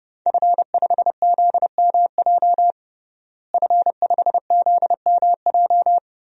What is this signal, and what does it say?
Morse code, heard as a keyed tone, about 32 wpm, 700 Hz. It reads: F5ZMJ F5ZMJ
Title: DXCC-100 Call Signs - 5 Characters - Repeated 2x - 30wpm